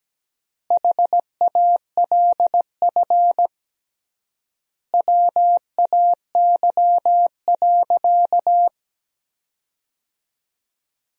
Morse code, heard as a keyed tone, about 17 wpm, 700 Hz. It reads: HALF WAY.
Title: 3x Word-Spacing - Sentences from Top 400 Words - 17wpm